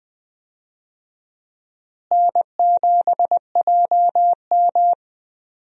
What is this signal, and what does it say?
Morse code, heard as a keyed tone, about 20 wpm, 700 Hz. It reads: N7JM
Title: USA Call Signs - 4 Characters - Repeated 3x - 20wpm